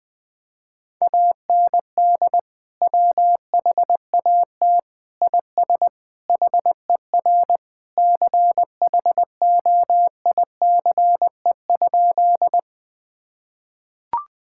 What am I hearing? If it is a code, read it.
Morse code, heard as a keyed tone, about 20 wpm, 700 Hz. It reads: AND WHAT IS HER CHOICE?
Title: Sentences from Top 1000 Words - Review Part 6 - 20wpm